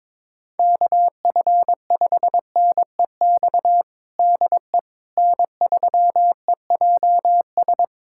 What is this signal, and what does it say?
Morse code, heard as a keyed tone, about 22 wpm, 700 Hz. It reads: KF5NEX DE N3EJS
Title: USA Callsign DE Callsign 22wpm